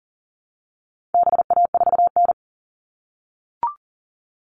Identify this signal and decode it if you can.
Morse code, heard as a keyed tone, about 40 wpm, 700 Hz. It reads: BA4N
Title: All Cut Numbers - 4-Digit 40wpm